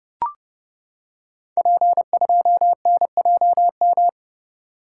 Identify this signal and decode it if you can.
Morse code, heard as a keyed tone, about 30 wpm, 700 Hz. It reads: P2NJM
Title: Code Groups - 5 Letter:Numbers - 30wpm